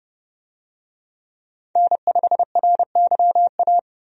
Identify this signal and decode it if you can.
Morse code, heard as a keyed tone, about 30 wpm, 700 Hz. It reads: N5RYA
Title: USA Call Signs - 5 Characters - 30wpm